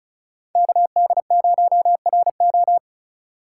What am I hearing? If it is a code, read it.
Morse code, heard as a keyed tone, about 36 wpm, 700 Hz. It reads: KD0RO
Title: USA Call Signs - 5 Characters - 3x Character Spacing - 35wpm